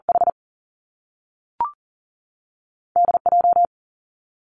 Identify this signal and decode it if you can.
Morse code, heard as a keyed tone, about 41 wpm, 700 Hz. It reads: H D J
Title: Two Challenging Letter-Number Combinations 40wpm